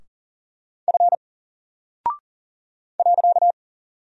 Morse code, heard as a keyed tone, about 40 wpm, 700 Hz. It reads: F .
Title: Single Character (Letter:Number:Punctuation:Prosign) 40wpm